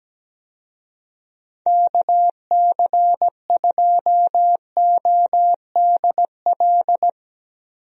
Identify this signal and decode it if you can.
Morse code, heard as a keyed tone, about 17 wpm, 700 Hz. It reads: KC2ODL
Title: USA Call Signs - 6 Characters - Repeated 2x - 17wpm